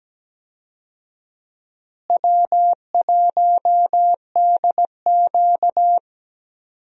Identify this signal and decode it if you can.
Morse code, heard as a keyed tone, about 17 wpm, 700 Hz. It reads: W1DQ
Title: USA Call Signs - 4 Characters - Repeated 2x - 17wpm